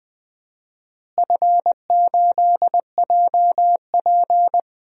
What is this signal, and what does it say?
Morse code, heard as a keyed tone, about 20 wpm, 700 Hz. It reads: F8JP
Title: DXCC-100 Call Signs - 4 Characters - Repeated 2x - 20wpm